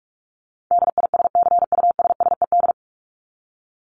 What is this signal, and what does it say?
Morse code, heard as a keyed tone, about 44 wpm, 700 Hz. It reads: DISCUSSED
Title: ED Suffix - Top 100 Words with Base Words 45wpm